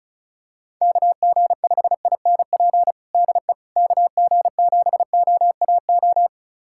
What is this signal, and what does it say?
Morse code, heard as a keyed tone, about 34 wpm, 700 Hz. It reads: KG5INP DE KG7OAO
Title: USA Callsign DE Callsign - 6 Characters - 35wpm